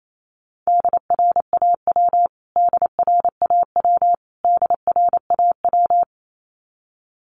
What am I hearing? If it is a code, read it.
Morse code, heard as a keyed tone, about 28 wpm, 700 Hz. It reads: DRAW DRAW DRAW
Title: Top 500 Words Repeated 28wpm